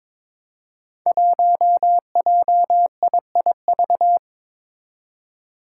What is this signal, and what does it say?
Morse code, heard as a keyed tone, about 22 wpm, 700 Hz. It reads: 1JIIV
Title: Five Challenging Letter-Number Combinations 22wpm